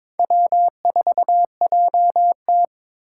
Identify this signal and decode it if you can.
Morse code, heard as a keyed tone, about 26 wpm, 700 Hz. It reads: W4JT
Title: Speed Racing - USA 1x2 Call Signs 22wpm